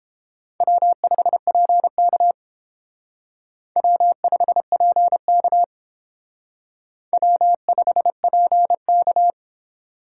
Morse code, heard as a keyed tone, about 30 wpm, 700 Hz. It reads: W5PK W5PK W5PK